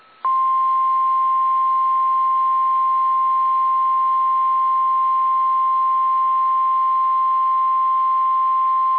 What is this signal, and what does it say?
Emergency Alert System (SAME) tones (the 1050 Hz attention tone). It is heard from 0.3 to 8.8 s.